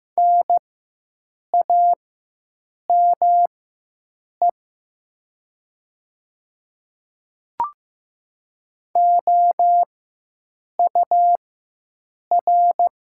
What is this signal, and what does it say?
Morse code, heard as a keyed tone, about 16 wpm, spaced as for 7 wpm, 700 Hz. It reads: NAME OUR
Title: 4x Character Spacing - Top 500 Words - 15wpm